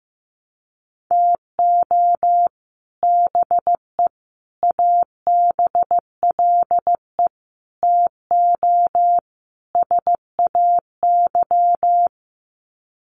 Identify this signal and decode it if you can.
Morse code, heard as a keyed tone, about 15 wpm, 700 Hz. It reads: TO BE ABLE TO SAY
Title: Sets of 5 Words Encore2 15wpm